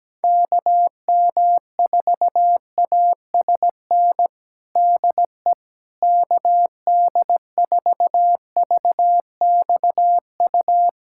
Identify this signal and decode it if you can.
Morse code, heard as a keyed tone, about 17 wpm, 700 Hz. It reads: KM4ASN DE KD4VXU
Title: USA Callsign DE Callsign - 6 Characters - 17wpm